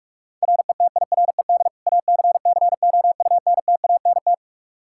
Morse code, heard as a keyed tone, about 46 wpm, 700 Hz. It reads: RETIRED ACCOUNTANT
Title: Practice QSO - Exchange 3 - More Info - 45wpm